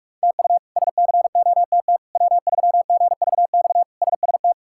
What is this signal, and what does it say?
Morse code, heard as a keyed tone, about 45 wpm, 700 Hz. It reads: TU SCOTT W3GVX SST
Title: K1USN SST QSO Practice - 45wpm